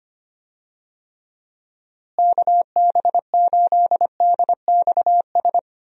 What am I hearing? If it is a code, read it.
Morse code, heard as a keyed tone, about 25 wpm, 700 Hz. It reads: KB8DXS
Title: USA Call Signs - 6 Characters - Repeated 2x - 25wpm